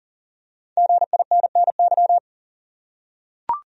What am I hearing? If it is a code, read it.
Morse code, heard as a keyed tone, about 40 wpm, 700 Hz. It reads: GINNY
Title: USA Common Names - Max 5 Letters - 40wpm